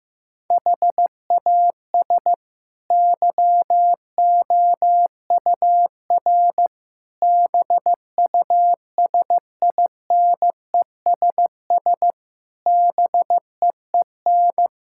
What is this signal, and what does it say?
Morse code, heard as a keyed tone, about 16 wpm, 700 Hz. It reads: HAS YOUR BUSINESS BEEN
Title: Sentences from Top 1000 Words - Review Part 6 - 15wpm